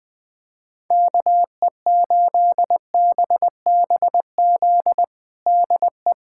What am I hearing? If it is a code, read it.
Morse code, heard as a keyed tone, about 19 wpm, 700 Hz. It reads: KE8BBZ DE
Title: USA Callsign DE Callsign - 6 Characters - 20wpm